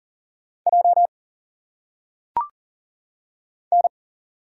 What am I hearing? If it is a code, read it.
Morse code, heard as a keyed tone, about 40 wpm, 700 Hz. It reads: J N
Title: Single Letters - Phonetic Pronunciation 40wpm